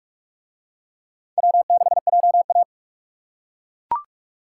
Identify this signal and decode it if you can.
Morse code, heard as a keyed tone, about 42 wpm, 700 Hz. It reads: W6JA